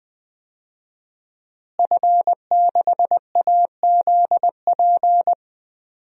Morse code, heard as a keyed tone, about 20 wpm, 700 Hz. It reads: F6AZP
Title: DXCC-100 Call Signs - 5 Characters - Repeated 2x - 20wpm